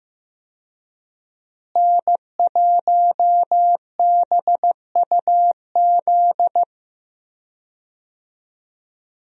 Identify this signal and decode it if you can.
Morse code, heard as a keyed tone, about 15 wpm, 700 Hz. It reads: N1BUZ
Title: USA Call Signs - 5 Characters - Repeated 2x - 15wpm